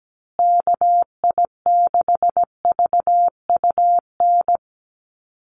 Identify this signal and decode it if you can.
Morse code, heard as a keyed tone, about 17 wpm, 700 Hz. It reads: KI6VUN
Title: USA Call Signs Repeated 2x 17wpm